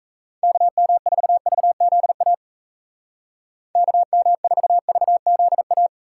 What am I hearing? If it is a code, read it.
Morse code, heard as a keyed tone, about 35 wpm, 700 Hz. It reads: KM4VZA KM4VZA